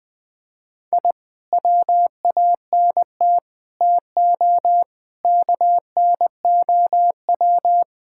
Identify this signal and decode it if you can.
Morse code, heard as a keyed tone, about 21 wpm, 700 Hz. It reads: I WANT TO KNOW
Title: Sentences with 2-Word Phrases - 20wpm